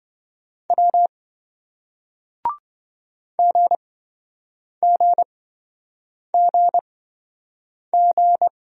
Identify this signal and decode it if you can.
Morse code, heard as a keyed tone, about 27 wpm, spaced as for 7 wpm, 700 Hz. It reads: W GGGG